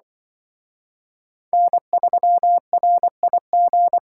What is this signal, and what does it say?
Morse code, heard as a keyed tone, about 24 wpm, 700 Hz. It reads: N3RIG
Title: Speed Racing - USA 1x3 Call Signs 20wpm